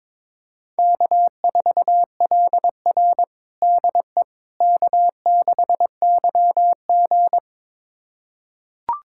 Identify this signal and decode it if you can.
Morse code, heard as a keyed tone, about 22 wpm, 700 Hz. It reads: K4LR DE K6YG
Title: USA Callsign DE Callsign - 4 Characters - 22wpm